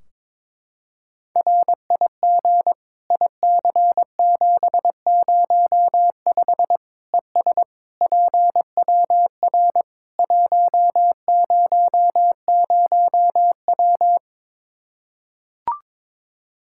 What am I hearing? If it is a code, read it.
Morse code, heard as a keyed tone, about 22 wpm, 700 Hz. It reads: RIG IC705 ES PWR 100W
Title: Practice QSO - Exchange 2 - Info - 22wpm